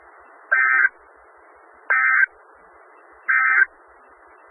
Emergency Alert System (SAME) tones, heard as harsh buzzing bursts. It is heard from 0.4 to 3.7 s.
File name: Skywarn/2013/2013-03-18-Newton County WA4ASI Repeater